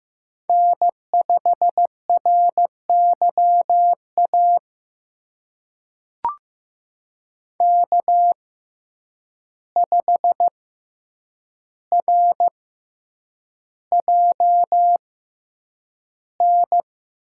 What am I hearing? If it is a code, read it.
Morse code, heard as a keyed tone, about 15 wpm, spaced as for 8 wpm, 700 Hz. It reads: N5RYA K5RJN
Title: USA Call Signs - 5 Characters - 6x Character Spacing - 15wpm